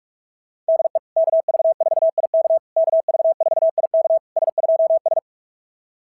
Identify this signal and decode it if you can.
Morse code, heard as a keyed tone, about 44 wpm, 660 Hz. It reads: DE KV4IK KV4IK S2S
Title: SOTA QSO Practice 45wpm